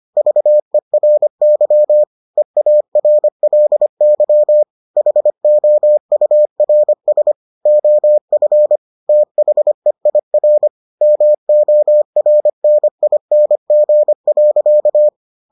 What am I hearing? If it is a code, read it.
Morse code, heard as a keyed tone, about 25 wpm, 600 Hz. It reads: VERY EARLY HOURS OF THEIR MORNING.